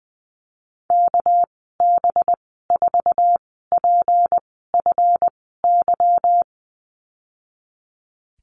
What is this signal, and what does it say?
Morse code, heard as a keyed tone, about 20 wpm, spaced as for 15 wpm, 700 Hz. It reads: KB4PFY
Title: 219 Lesson 35g - Number 0 - Call Signs 20wpm